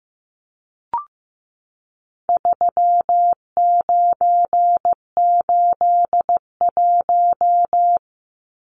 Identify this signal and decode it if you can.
Morse code, heard as a keyed tone, about 15 wpm, 700 Hz. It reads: 3981
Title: Numbers - 4-Digit 15wpm